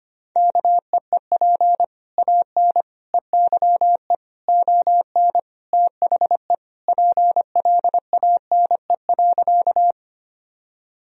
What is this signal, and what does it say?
Morse code, heard as a keyed tone, about 25 wpm, 700 Hz. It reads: KEEP AN EYE ON THE PLANE.
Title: Sentences from Top 1000 Words - Review Part 3 - 25wpm